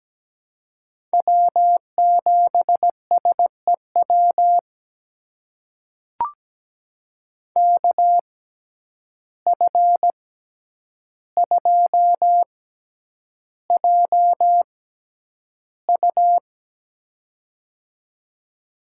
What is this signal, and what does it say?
Morse code, heard as a keyed tone, about 17 wpm, spaced as for 9 wpm, 700 Hz. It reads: W7SEW KF2JU
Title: USA Call Signs - 5 Characters - 6x Character Spacing - 17wpm